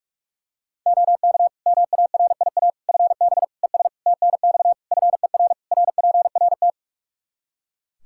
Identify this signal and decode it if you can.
Morse code, heard as a keyed tone, about 41 wpm, 700 Hz. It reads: OK MARIA FB ES TNX FER RPRT
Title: Practice QSO - Exchange 2 - Info - 45wpm